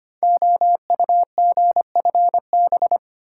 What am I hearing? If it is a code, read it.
Morse code, heard as a keyed tone, about 25 wpm, 700 Hz. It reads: OUGFB
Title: Code Groups - 5 Letters - 25wpm